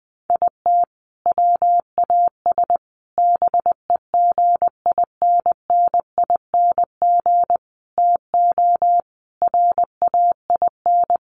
Morse code, heard as a keyed tone, about 20 wpm, 700 Hz. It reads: IT WAS BEGINNING TO RAIN
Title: Sentences from Top 800 Words 20wpm